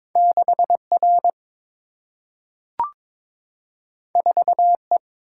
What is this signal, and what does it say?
Morse code, heard as a keyed tone, about 22 wpm, 700 Hz. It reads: 6 R 4 E
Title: Single-Letter Single-Number 22wpm